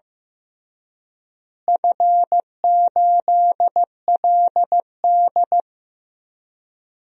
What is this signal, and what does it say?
Morse code, heard as a keyed tone, about 16 wpm, 700 Hz. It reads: F8LD
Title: DXCC-100 Call Signs - 4 Characters - Repeated 3x - 15wpm